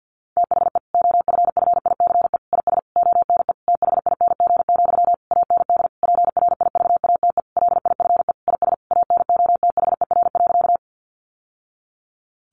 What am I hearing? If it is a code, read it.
Morse code, heard as a keyed tone, about 50 wpm, 700 Hz. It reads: THE OFFICE IS ONE THING, AND PRIVATE LIFE IS ANOTHER.
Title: Sentences from Top 1000 Words 50wpm